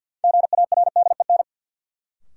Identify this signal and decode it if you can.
Morse code, heard as a keyed tone, about 53 wpm, 700 Hz. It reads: GARDEN
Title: Sentences from Top 700 Words - New Word Drill - 50wpm